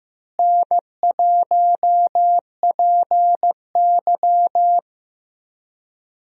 Morse code, heard as a keyed tone, about 15 wpm, 700 Hz. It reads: N1PY
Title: USA Call Signs - 4 Characters - Repeated 2x - 15wpm